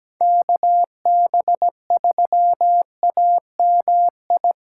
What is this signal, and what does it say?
Morse code, heard as a keyed tone, about 17 wpm, 700 Hz. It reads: KB3AMI
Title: USA Callsign DE Callsign - 6 Characters - 17wpm